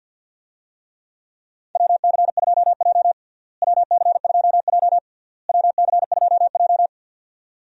Morse code, heard as a keyed tone, about 49 wpm, 700 Hz. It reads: WC2J WC2J WC2J
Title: USA Call Signs - 4 Characters - Repeated 3x - 50wpm